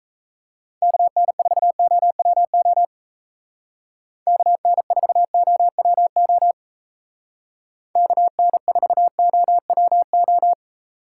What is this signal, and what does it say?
Morse code, heard as a keyed tone, about 33 wpm, 700 Hz. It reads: KN4OWO KN4OWO KN4OWO
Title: Speed Racing - USA 2x3 Call Signs 28wpm